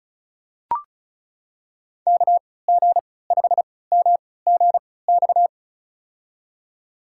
Morse code, heard as a keyed tone, about 36 wpm, spaced as for 20 wpm, 700 Hz. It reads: KG5MGX
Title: USA Call Signs - 3x Character Spacing - 35wpm